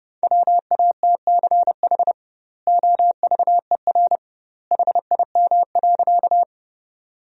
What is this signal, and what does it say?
Morse code, heard as a keyed tone, about 32 wpm, 700 Hz. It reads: WATCH OVER HIM.
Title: 2x Word-Spacing - Sentences from Top 300 Words - 30wpm